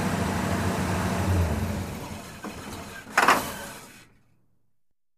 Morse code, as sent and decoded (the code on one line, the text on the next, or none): none